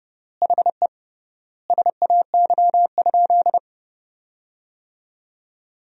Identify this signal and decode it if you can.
Morse code, heard as a keyed tone, about 30 wpm, 700 Hz. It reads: HE SAY?
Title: 3x Word-Spacing - Sentences from Top 300 Words - 30wpm